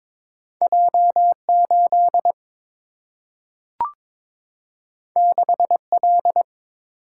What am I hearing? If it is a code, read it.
Morse code, heard as a keyed tone, about 22 wpm, 700 Hz. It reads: J 8 6 L
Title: Single-Letter Single-Number 22wpm